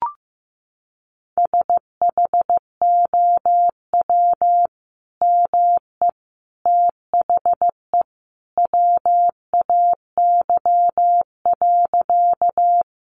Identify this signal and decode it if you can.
Morse code, heard as a keyed tone, about 15 wpm, 700 Hz. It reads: SHOW ME THE WAY.
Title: Sentences from Top 200 Words 15wpm